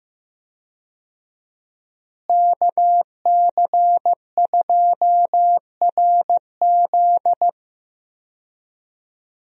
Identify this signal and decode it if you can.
Morse code, heard as a keyed tone, about 15 wpm, 700 Hz. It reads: KC2RZ
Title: USA Call Signs - 5 Characters - Repeated 2x - 15wpm